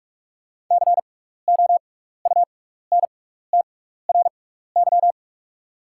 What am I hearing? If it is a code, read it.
Morse code, heard as a keyed tone, about 48 wpm, spaced as for 16 wpm, 690 Hz. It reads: COUNTRY
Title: 6x Character Spacing - Top 500 Words - 45wpm